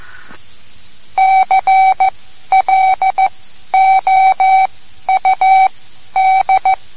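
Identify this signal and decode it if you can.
Morse code, heard as a keyed tone, about 15 wpm, spaced as for 11 wpm, 760 Hz. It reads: CLOUD
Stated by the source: Morse code lesson, third lesson in a series for learning the Morse alphabet.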